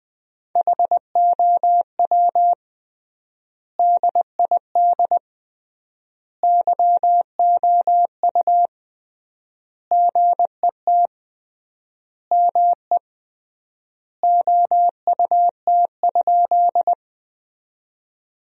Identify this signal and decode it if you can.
Morse code, heard as a keyed tone, about 20 wpm, 700 Hz. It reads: HOW DID YOU GET ME OUT?
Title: 3x Word-Spacing - Sentences from Top 200 Words - 20wpm